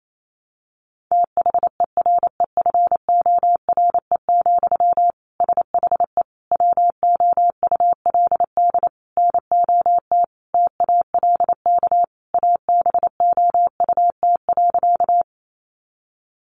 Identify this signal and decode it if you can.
Morse code, heard as a keyed tone, about 28 wpm, 700 Hz. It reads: THEREFORE, SHE WOULD NOT TALK ABOUT.
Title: Sentences from Top 800 Words 28wpm